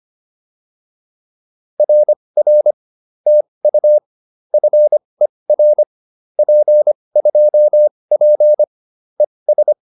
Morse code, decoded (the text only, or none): RR TU FER P2P ES